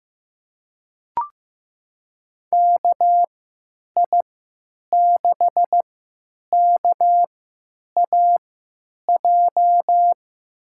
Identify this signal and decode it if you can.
Morse code, heard as a keyed tone, about 16 wpm, spaced as for 9 wpm, 700 Hz. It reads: KI6KAJ
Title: USA Call Signs - 3x Character Spacing - 15wpm